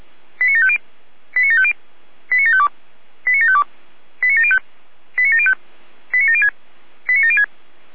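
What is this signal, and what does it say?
Five-tone selective call (ZVEI), heard as quick runs of five tones, tones of about 70 ms. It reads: 89750 89750 89752 89752 89705 89705 89706 89706
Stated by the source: ZVEI five-tone call 89533 received by SDR.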